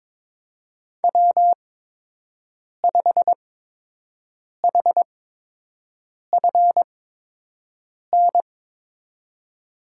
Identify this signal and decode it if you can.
Morse code, heard as a keyed tone, about 22 wpm, spaced as for 6 wpm, 700 Hz. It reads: W5HFN
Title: USA Call Signs - 8x Character Spacing - 22wpm